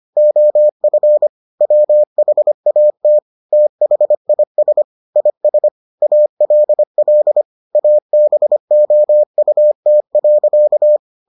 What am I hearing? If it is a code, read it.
Morse code, heard as a keyed tone, about 25 wpm, 600 Hz. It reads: OF WHAT THIS IS ALL ABOUT.